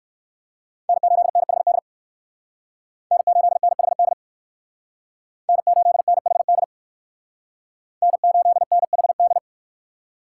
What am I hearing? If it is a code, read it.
Morse code, heard as a keyed tone, about 57 wpm, 700 Hz. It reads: N8NHD N8NHD N8NHD N8NHD